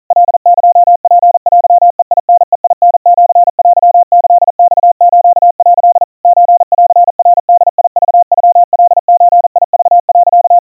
Morse code (Japanese Wabun code), heard as a keyed tone, about 40 wpm, 700 Hz. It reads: カエツテベタベタシテキマス」 ソンナボクノカゾク、
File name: kawa / Morse sample / kamyu 150 cpm 700 Hz Crispy